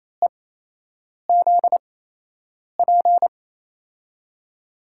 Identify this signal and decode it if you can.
Morse code, heard as a keyed tone, about 28 wpm, spaced as for 8 wpm, 700 Hz. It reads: EZP
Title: USA Call Signs - 6 Characters - 8x Character Spacing - 28wpm